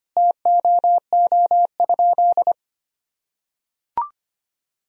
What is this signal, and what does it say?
Morse code, heard as a keyed tone, about 25 wpm, 700 Hz. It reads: TOO?
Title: Sentences from Top 1000 Words - Review Part 4 - 25wpm